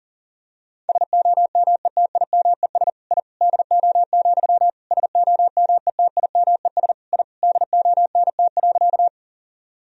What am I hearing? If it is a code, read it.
Morse code, heard as a keyed tone, about 40 wpm, 700 Hz. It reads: SOMETIMES I DO, SOMETIMES I DONT.
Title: Sentences from Top 1000 Words - Review Part 3 - 40wpm